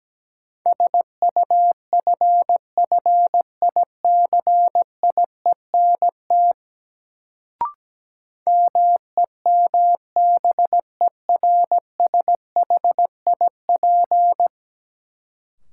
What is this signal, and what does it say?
Morse code, heard as a keyed tone, about 17 wpm, 700 Hz. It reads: SUFFICIENT MEMBERSHIP
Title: Sentences from Top 1400 Words - New Word Drill Alt - 17wpm